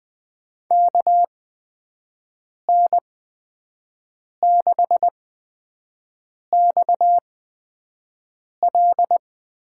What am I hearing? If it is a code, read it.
Morse code, heard as a keyed tone, about 20 wpm, spaced as for 5 wpm, 700 Hz. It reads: KN6XL
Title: USA Call Signs - 5 Characters - 8x Character Spacing - 20wpm